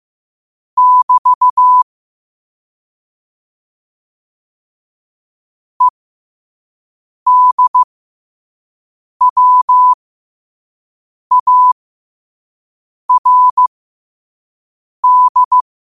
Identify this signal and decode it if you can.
Morse code, heard as a keyed tone, about 15 wpm, spaced as for 5 wpm, 1000 Hz. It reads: <BT> EDWARD
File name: qotdfiles / 5wpm